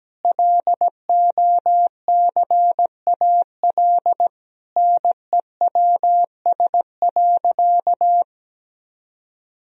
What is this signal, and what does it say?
Morse code, heard as a keyed tone, about 17 wpm, 700 Hz. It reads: LOCAL NEWS.
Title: Sentences from Top 1000 Words - Review Part 3 - 17wpm